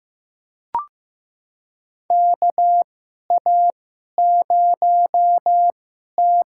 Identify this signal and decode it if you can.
Morse code, heard as a keyed tone, about 15 wpm, spaced as for 11 wpm, 700 Hz. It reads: KA0T
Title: USA Call Signs - 6 Characters - 2x Character Spacing - 15wpm